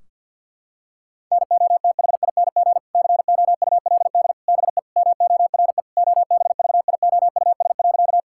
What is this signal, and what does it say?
Morse code, heard as a keyed tone, about 50 wpm, 700 Hz. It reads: NOTHING COULD BE MORE OBVIOUS.
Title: Sentences from Top 1600 Words 50wpm